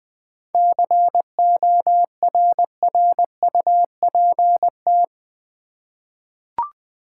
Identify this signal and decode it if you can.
Morse code, heard as a keyed tone, about 20 wpm, 700 Hz. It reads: CORRUPT
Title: 7 Letter Words - First 3 Letters Given - 20wpm